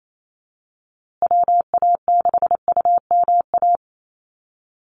Morse code, heard as a keyed tone, about 28 wpm, 700 Hz. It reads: WA6UMA
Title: USA Call Signs Repeated 3x 28wpm